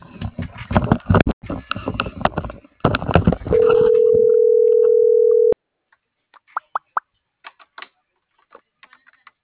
Ringback tone (North America), 440 + 480 Hz.